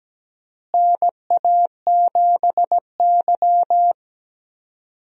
Morse code, heard as a keyed tone, about 17 wpm, 700 Hz. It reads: NA7Y
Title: USA Call Signs - 4 Characters - Repeated 3x - 17wpm